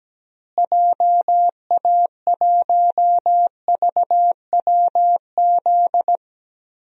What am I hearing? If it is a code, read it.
Morse code, heard as a keyed tone, about 17 wpm, 700 Hz. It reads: JA1VWZ
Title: DXCC-100 Call Signs - 6 Characters - Repeated 3x - 17wpm